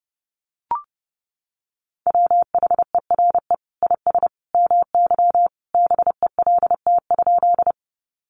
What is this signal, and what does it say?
Morse code, heard as a keyed tone, about 30 wpm, 700 Hz. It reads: WHERE IS MY BELT?
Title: Sentences from Top 1400 Words 30wpm